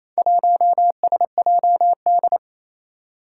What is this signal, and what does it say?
Morse code, heard as a keyed tone, about 28 wpm, 700 Hz. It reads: 1SJD
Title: Four Challenging Letter-Number Combinations 28wpm